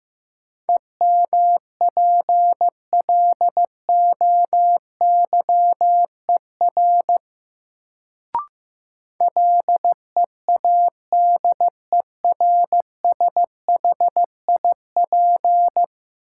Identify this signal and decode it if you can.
Morse code, heard as a keyed tone, about 15 wpm, 700 Hz. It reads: EMPLOYER LEADERSHIP
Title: Sentences from Top 1600 Words - New Word Drill Alt - 15wpm